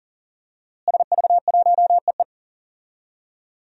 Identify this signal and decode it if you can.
Morse code, heard as a keyed tone, about 37 wpm, 700 Hz. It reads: SV1EE
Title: DXCC-100 Call Signs - 5 Characters - Repeated 2x - 40wpm